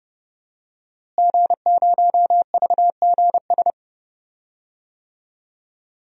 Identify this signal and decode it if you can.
Morse code, heard as a keyed tone, about 29 wpm, 700 Hz. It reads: G0VGS
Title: DXCC-100 Call Signs - 5 Characters - Repeated 3x - 30wpm